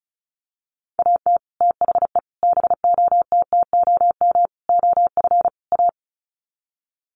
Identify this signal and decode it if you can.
Morse code, heard as a keyed tone, about 34 wpm, 700 Hz. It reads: AT THE BOTTOM OF A
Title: Sets of 5 Words Encore2 35wpm